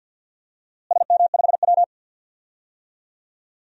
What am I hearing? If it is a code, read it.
Morse code, heard as a keyed tone, about 51 wpm, 700 Hz. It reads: SM5W
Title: DXCC-100 Call Signs - 4 Characters - Repeated 3x - 50wpm